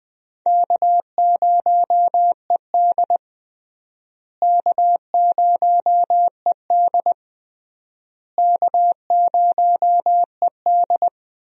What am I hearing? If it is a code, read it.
Morse code, heard as a keyed tone, about 21 wpm, 700 Hz. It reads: K0ED K0ED K0ED
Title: USA Call Signs - 4 Characters - Repeated 3x - 20wpm